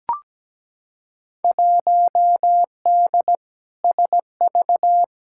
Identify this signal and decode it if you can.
Morse code, heard as a keyed tone, about 17 wpm, 700 Hz. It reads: 1D SV
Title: Field Day Exchanges - 17wpm